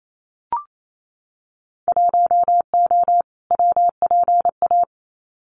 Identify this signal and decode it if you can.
Morse code, heard as a keyed tone, about 28 wpm, 700 Hz. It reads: 1O WPA
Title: Winter Field Day Exchanges 28wpm